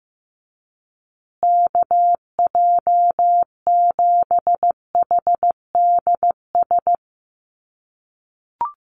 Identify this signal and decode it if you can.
Morse code, heard as a keyed tone, about 15 wpm, 700 Hz. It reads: KJ7HDS
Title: USA Call Signs Repeated 3x 15wpm